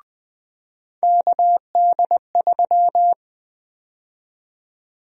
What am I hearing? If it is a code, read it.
Morse code, heard as a keyed tone, about 20 wpm, 700 Hz. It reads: KD3
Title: USA Call Sign Prefixes - 20wpm